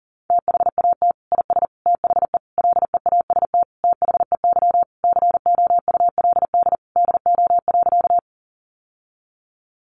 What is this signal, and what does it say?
Morse code, heard as a keyed tone, about 40 wpm, 700 Hz. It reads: THAT IS THE LEAST THEY COULD DO.
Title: Sentences from Top 700 Words 40wpm